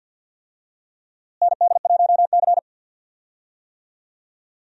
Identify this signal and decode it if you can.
Morse code, heard as a keyed tone, about 46 wpm, 700 Hz. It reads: ND1C
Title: USA Call Signs - 4 Characters - Repeated 3x - 50wpm